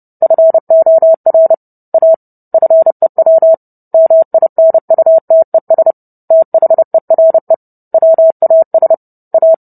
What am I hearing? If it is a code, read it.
Morse code, heard as a keyed tone, about 30 wpm, 650 Hz. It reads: FOR A FEW MINUTES THERE WAS A